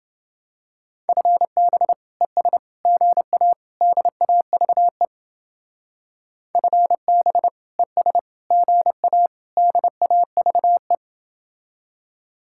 Morse code, heard as a keyed tone, about 29 wpm, 700 Hz. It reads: FB ES GA DAVE FB ES GA DAVE